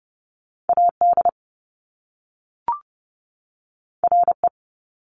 Morse code, heard as a keyed tone, about 30 wpm, 700 Hz. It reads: AD RE
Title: Top Two Letter Combinations in Words 30wpm